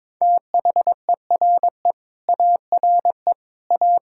Morse code, heard as a keyed tone, about 22 wpm, 700 Hz. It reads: THERE ARE A
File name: practice / 22wpm / Sets of 8 Words - 22wpm